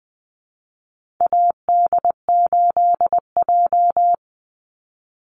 Morse code, heard as a keyed tone, about 20 wpm, 700 Hz. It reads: AD8J
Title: Call Signs - 20wpm